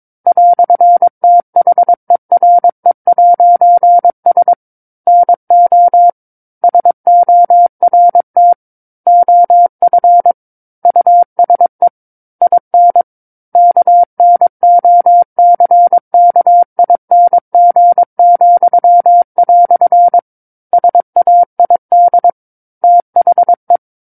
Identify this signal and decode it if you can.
Morse code, heard as a keyed tone, about 23 wpm, 700 Hz. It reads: "THERE'S NO SORT OF USE IN KNOCKING," SAID THE
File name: 06-Chapter-06 22 WPM